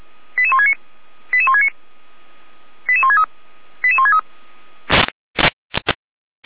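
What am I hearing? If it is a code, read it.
Five-tone selective call (ZVEI), heard as quick runs of five tones, tones of about 80 ms. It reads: 88169 88169 88163 88163
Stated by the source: ZVEI five-tone call 88168 received by SDR, channel 455.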